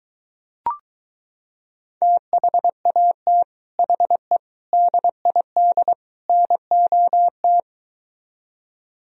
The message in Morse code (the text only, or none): THAT HE DID NOT